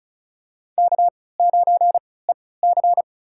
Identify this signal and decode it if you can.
Morse code, heard as a keyed tone, about 34 wpm, spaced as for 20 wpm, 700 Hz. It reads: K9EC
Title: USA Call Signs - 4 Characters - 3x Character Spacing - 35wpm